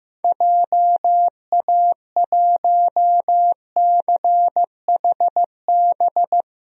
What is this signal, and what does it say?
Morse code, heard as a keyed tone, about 15 wpm, 700 Hz. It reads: JA1CHB
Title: DXCC-100 Call Signs - 6 Characters - Repeated 3x - 15wpm